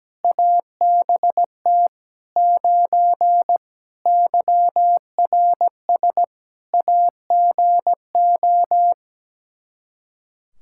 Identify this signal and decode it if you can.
Morse code, heard as a keyed tone, about 17 wpm, 700 Hz. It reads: ABT 9 YRS AGO
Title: Practice QSO - Exchange 3 - More Info - 17wpm